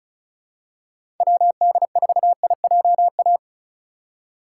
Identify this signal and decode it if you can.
Morse code, heard as a keyed tone, about 34 wpm, 700 Hz. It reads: WD4IJA
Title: USA Call Signs - 6 Characters - 3x Character Spacing - 35wpm